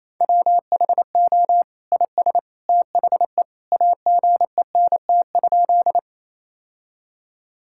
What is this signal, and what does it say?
Morse code, heard as a keyed tone, about 28 wpm, 700 Hz. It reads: WHO IS THE AGENT?
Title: Sentences from Top 1600 Words 28wpm